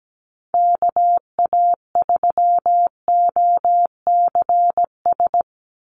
Morse code, heard as a keyed tone, about 17 wpm, 700 Hz. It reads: KA3OCS
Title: Call Signs - 17wpm